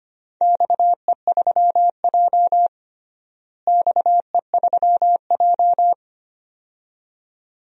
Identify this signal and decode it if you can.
Morse code, heard as a keyed tone, about 25 wpm, 700 Hz. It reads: XE3J XE3J
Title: DXCC-100 Call Signs - 4 Characters - Repeated 2x - 25wpm